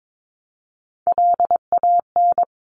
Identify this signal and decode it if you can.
Morse code, heard as a keyed tone, about 22 wpm, 700 Hz. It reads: LAN
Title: Top Three Letter Combinations in Words 22wpm